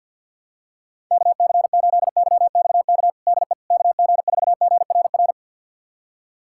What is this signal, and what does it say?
Morse code, heard as a keyed tone, about 51 wpm, 700 Hz. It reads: KC8YXK DE KG4GRR
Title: USA Callsign DE Callsign - 6 Characters - 50wpm